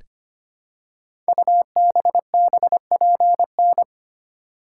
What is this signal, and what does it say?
Morse code, heard as a keyed tone, about 25 wpm, 700 Hz. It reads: UBBPN
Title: Code Groups - 5 Letter:Numbers - 25wpm